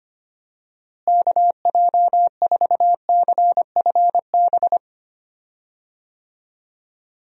Morse code, heard as a keyed tone, about 25 wpm, 700 Hz. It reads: KJ4CFB
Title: USA Call Signs - 6 Characters - Repeated 2x - 25wpm